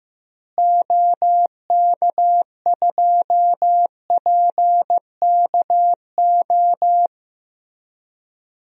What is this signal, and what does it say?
Morse code, heard as a keyed tone, about 15 wpm, 700 Hz. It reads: OK2PKO
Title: DXCC-100 Call Signs - 6 Characters - Repeated 2x - 15wpm